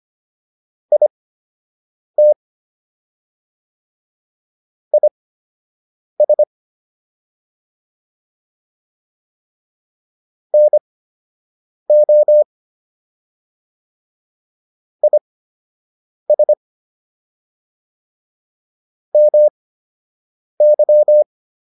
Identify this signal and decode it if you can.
Morse code, heard as a keyed tone, about 25 wpm, spaced as for 7 wpm, 600 Hz. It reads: IT IS NO IS MY